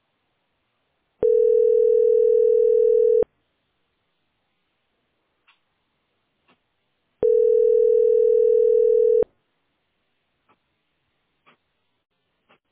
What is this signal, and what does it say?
Ringback tone (North America), 440 + 480 Hz.